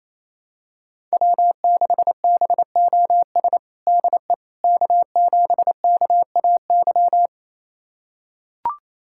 Morse code, heard as a keyed tone, about 28 wpm, 700 Hz. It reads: W6BOS DE K7KAY
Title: USA Callsign DE Callsign - 5 Characters - 28wpm